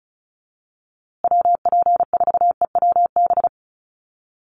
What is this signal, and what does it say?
Morse code, heard as a keyed tone, about 34 wpm, 700 Hz. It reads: WP4EWB
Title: USA Call Signs Repeated 2x 35wpm